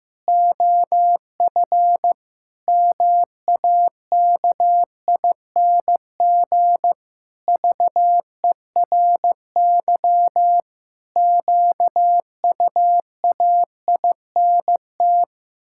Morse code, heard as a keyed tone, about 15 wpm, 700 Hz. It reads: OF MAKING VERY QUAINT
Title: Pangram Challenge v2 - 15wpm